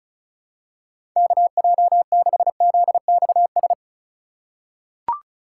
Morse code, heard as a keyed tone, about 36 wpm, 700 Hz. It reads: KJ6ZXS
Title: USA Call Signs - 3x Character Spacing - 35wpm